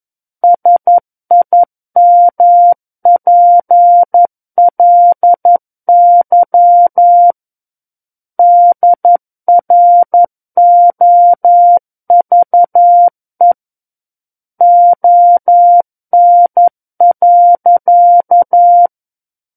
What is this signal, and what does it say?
Morse code, heard as a keyed tone, about 11 wpm, 700 Hz. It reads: SIMPLY DROVE ON.
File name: Book One / 02-Chapter 2 - 11 WPM